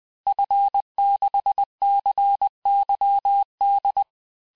Morse code, heard as a keyed tone, about 20 wpm, 800 Hz. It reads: F6CYD
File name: cours-du-mercredi / audios / cours-870-FIN-de-QSO-vitesse-19-MPM